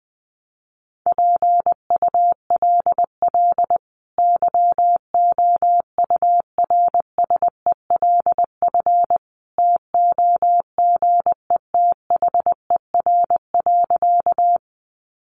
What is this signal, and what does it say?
Morse code, heard as a keyed tone, about 20 wpm, 700 Hz. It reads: PULL YOURSELF TOGETHER.
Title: Sentences from Top 600 Words 20wpm